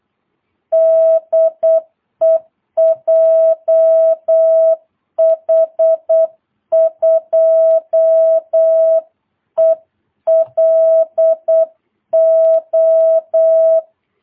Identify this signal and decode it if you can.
Morse code, heard as a keyed tone, about 8 wpm, 660 Hz. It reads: DE JH2ELO
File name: jh8cbh / 2022cw